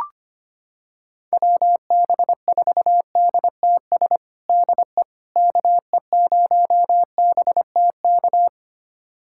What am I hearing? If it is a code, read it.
Morse code, heard as a keyed tone, about 25 wpm, 700 Hz. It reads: WB4DTS DE KE0BTK
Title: USA Callsign DE Callsign - 6 Characters - 25wpm